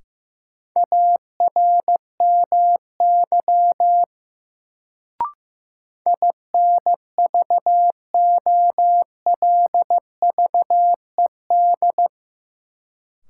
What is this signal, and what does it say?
Morse code, heard as a keyed tone, about 15 wpm, 700 Hz. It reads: ARMY INVOLVED
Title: Sentences from Top 800 Words - New Word Drill - 15wpm